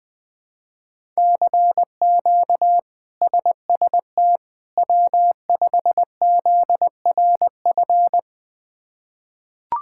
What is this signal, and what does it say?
Morse code, heard as a keyed tone, about 21 wpm, 700 Hz. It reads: CQ SST W5ZRF
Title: K1USN SST QSO Practice - 20wpm